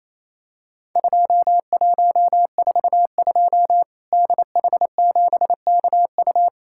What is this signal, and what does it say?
Morse code, heard as a keyed tone, about 28 wpm, 700 Hz. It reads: 2142 DH7KU
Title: Worked All Europe DX Contest - QTC Exchange - 28wpm